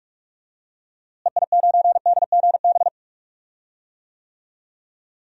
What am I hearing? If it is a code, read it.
Morse code, heard as a keyed tone, about 43 wpm, 700 Hz. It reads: EI9DGB
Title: DXCC-100 Call Signs - 6 Characters - Repeated 2x - 45wpm